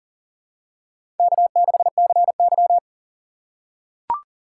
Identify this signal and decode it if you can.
Morse code, heard as a keyed tone, about 39 wpm, 700 Hz. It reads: K6CY
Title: USA Call Signs - 4 Characters - 4x Character Spacing - 40wpm